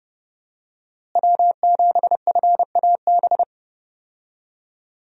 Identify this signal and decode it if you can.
Morse code, heard as a keyed tone, about 28 wpm, 700 Hz. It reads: W7FAB
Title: USA Call Signs - 5 Characters - Repeated 3x - 30wpm